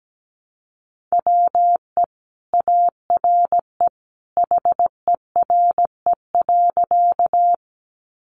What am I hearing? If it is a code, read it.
Morse code, heard as a keyed tone, about 17 wpm, 700 Hz. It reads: WE ARE HERE.
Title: Sentences from Top 200 Words 17wpm